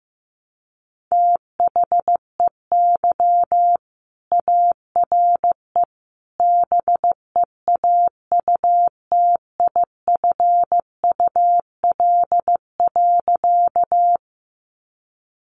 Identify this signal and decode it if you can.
Morse code, heard as a keyed tone, about 15 wpm, 700 Hz. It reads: THEY ARE BEAUTIFUL.
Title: Sentences from Top 900 Words 15wpm